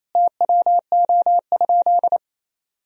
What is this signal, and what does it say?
Morse code, heard as a keyed tone, about 27 wpm, 700 Hz. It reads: TWO?
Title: Sentences from Top 1000 Words - Review Part 2 - 28wpm